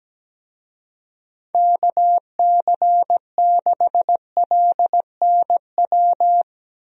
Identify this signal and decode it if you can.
Morse code, heard as a keyed tone, about 17 wpm, 700 Hz. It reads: KC6LNW
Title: USA Call Signs - 6 Characters - 17wpm